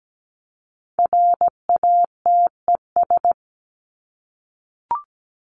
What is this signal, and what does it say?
Morse code, heard as a keyed tone, about 17 wpm, 700 Hz. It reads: RATES
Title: ES:S Suffix - Top 100 Words with Base Words 17wpm